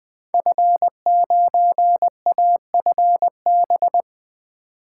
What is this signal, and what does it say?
Morse code, heard as a keyed tone, about 20 wpm, 700 Hz. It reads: F9AFB
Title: Code Groups - 5 Letter:Numbers - 20wpm